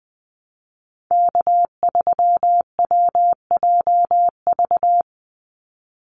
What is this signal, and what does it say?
Morse code, heard as a keyed tone, about 20 wpm, 700 Hz. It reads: K3WJV
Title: Call Signs Repeated 3x - 20wpm